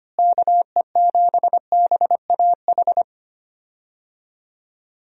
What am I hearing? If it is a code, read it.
Morse code, heard as a keyed tone, about 25 wpm, 700 Hz. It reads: KE7BAH
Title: USA Call Signs - 6 Characters - Repeated 2x - 25wpm